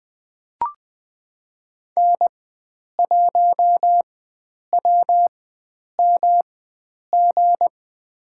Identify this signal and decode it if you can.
Morse code, heard as a keyed tone, about 20 wpm, spaced as for 9 wpm, 700 Hz. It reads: N1WMG
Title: USA Call Signs - 5 Characters - 4x Character Spacing - 20wpm